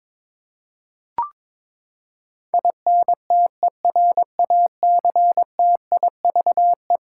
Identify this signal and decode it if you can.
Morse code, heard as a keyed tone, about 22 wpm, 700 Hz. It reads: INTERACTIVE
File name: practice / 22wpm / Eleven Letter Words - 22wpm